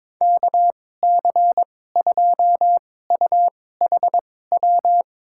Morse code, heard as a keyed tone, about 22 wpm, spaced as for 16 wpm, 700 Hz. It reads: KC2UHW
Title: USA Call Signs - 2x Character Spacing - 22wpm